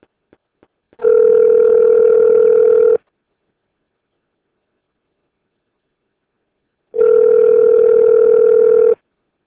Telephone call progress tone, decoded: ringback tone